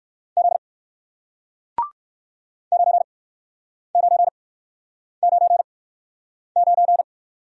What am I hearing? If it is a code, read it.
Morse code, heard as a keyed tone, about 54 wpm, spaced as for 9 wpm, 700 Hz. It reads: Z 9999
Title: Speed-Racing Single Letter-Number 45wpm